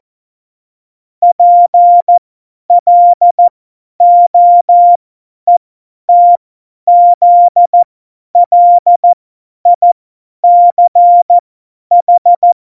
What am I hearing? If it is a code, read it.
Morse code, heard as a keyed tone, about 14 wpm, spaced as for 10 wpm, 700 Hz. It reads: PLOETZLICH
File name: Morsekurs / Witze / Witz 10 April 2018, 50 F70